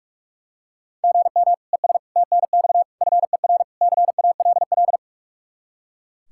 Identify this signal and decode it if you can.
Morse code, heard as a keyed tone, about 42 wpm, 700 Hz. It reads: GM ES TNX FER CALL
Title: Practice QSO - Exchange 1 - Beginning - 45wpm